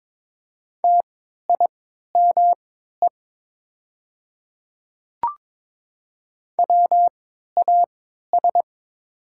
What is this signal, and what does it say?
Morse code, heard as a keyed tone, about 22 wpm, spaced as for 12 wpm, 700 Hz. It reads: TIME WAS
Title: 3x Character Spacing - Top 200 Words - 22wpm